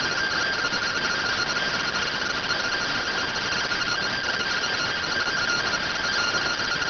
Two-tone teleprinter signal, about 170 Hz shift.